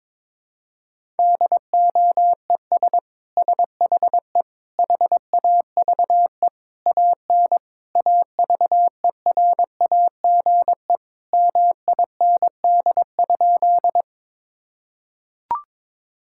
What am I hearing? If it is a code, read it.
Morse code, heard as a keyed tone, about 22 wpm, 700 Hz. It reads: DOES SHE HAVE AN AVERAGE MIND?
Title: Sentences from Top 1000 Words - Review Part 6 - 22wpm